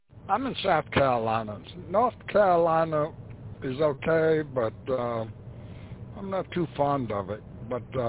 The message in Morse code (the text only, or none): none